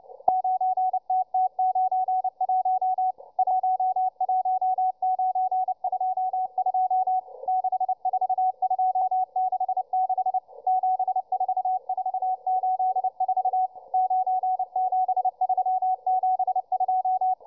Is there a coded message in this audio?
Yes: Morse code from 0.3 to 17.3 s, heard as a keyed tone, about 29 wpm, 750 Hz, with radio noise. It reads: 9TT91 11922 64266 74484 97372